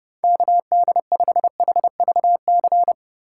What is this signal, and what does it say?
Morse code, heard as a keyed tone, about 29 wpm, 700 Hz. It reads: KD5HVC